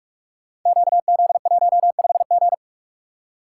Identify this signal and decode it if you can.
Morse code, heard as a keyed tone, about 41 wpm, 700 Hz. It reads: QZ15G
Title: Code Groups - 5 Letter:Numbers - 45wpm